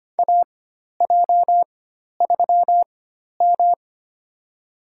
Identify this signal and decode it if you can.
Morse code, heard as a keyed tone, about 25 wpm, spaced as for 12 wpm, 700 Hz. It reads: AJ3M
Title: USA Call Signs - 4x Character Spacing - 25wpm